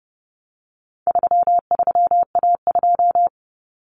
Morse code, heard as a keyed tone, about 30 wpm, 700 Hz. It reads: 33A2
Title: Common Cut Numbers - 4-Digit 30wpm